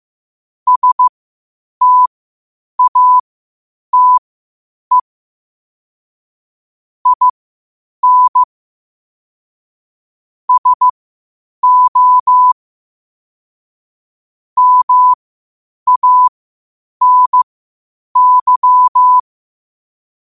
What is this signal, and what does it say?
Morse code, heard as a keyed tone, about 15 wpm, spaced as for 8 wpm, 1000 Hz. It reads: STATE IN SO MANY